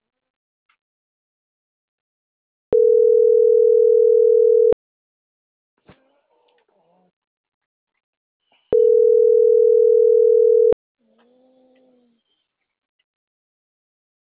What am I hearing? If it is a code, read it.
Ringback tone (North America), 440 + 480 Hz.